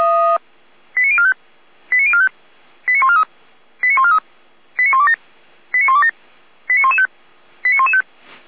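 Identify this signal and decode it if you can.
Five-tone selective call (ZVEI), heard as quick runs of five tones, tones of about 70 ms. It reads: 89045 89045 89143 89143 89127 89127 89115 89115
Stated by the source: ZVEI five-tone selective call received by SDR (455).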